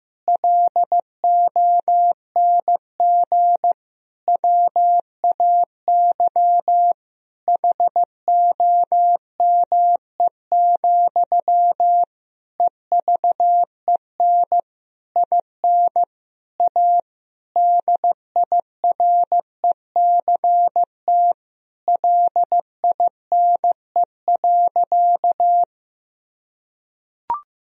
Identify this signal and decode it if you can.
Morse code, heard as a keyed tone, about 15 wpm, 700 Hz. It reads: LONG WAY HOME, EVEN IN A DIRECT LINE.
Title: Sentences from Top 1000 Words - Review Part 2 - 15wpm